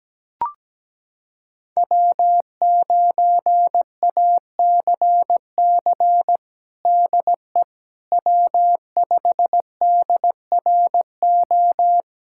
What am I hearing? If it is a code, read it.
Morse code, heard as a keyed tone, about 17 wpm, 700 Hz. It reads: W9ACC DE W5DRO
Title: USA Callsign DE Callsign - 5 Characters - 17wpm